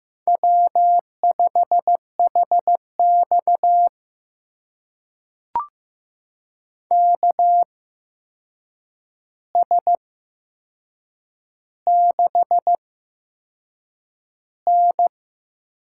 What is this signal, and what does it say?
Morse code, heard as a keyed tone, about 15 wpm, spaced as for 6 wpm, 700 Hz. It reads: W5HX KS6N
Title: USA Call Signs - 8x Character Spacing - 15wpm